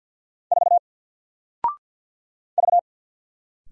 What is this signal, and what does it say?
Morse code, heard as a keyed tone, about 54 wpm, 700 Hz. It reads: 4 V
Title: Single Letter-Number Alt 50wpm